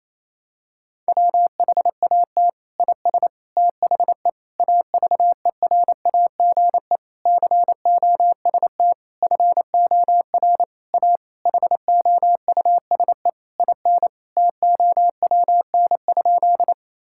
Morse code, heard as a keyed tone, about 28 wpm, 700 Hz. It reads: WHAT IS THE AVERAGE COST FOR A HOUSE IN TOWN?
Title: Sentences from Top 1000 Words - Review Part 6 - 28wpm